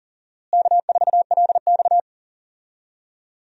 Morse code, heard as a keyed tone, about 37 wpm, 700 Hz. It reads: K4LX
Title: USA Call Signs - 4 Characters - Repeated 3x - 40wpm